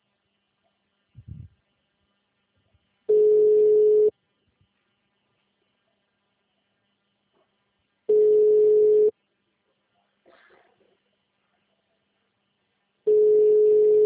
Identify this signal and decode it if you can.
Ringback tone (Europe, 425 Hz).
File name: out- -2032- - - .40